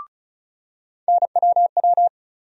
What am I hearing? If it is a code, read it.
Morse code, heard as a keyed tone, about 36 wpm, 700 Hz. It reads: NWW
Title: USA Call Sign Postfixes - 3 Letters 35wpm